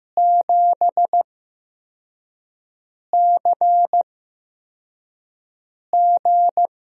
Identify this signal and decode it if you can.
Morse code, heard as a keyed tone, about 15 wpm, spaced as for 4 wpm, 700 Hz. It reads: 7CG
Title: USA Call Signs - 8x Character Spacing - 15wpm